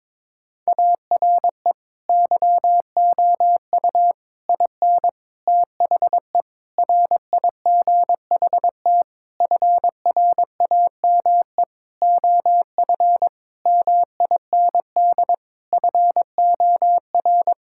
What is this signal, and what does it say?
Morse code, heard as a keyed tone, about 21 wpm, 700 Hz. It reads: ARE YOU IN THE RIGHT FRAME OF MIND FOR
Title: Sentences from Top 1000 Words - Review Part 6 - 22wpm